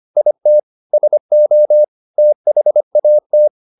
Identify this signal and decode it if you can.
Morse code, heard as a keyed tone, about 25 wpm, 600 Hz. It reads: IT SO THAT